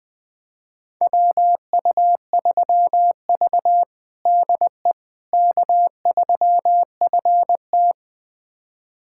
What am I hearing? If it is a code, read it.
Morse code, heard as a keyed tone, about 21 wpm, 700 Hz. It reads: WU3V DE K3FT
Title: USA Callsign DE Callsign - 4 Characters - 20wpm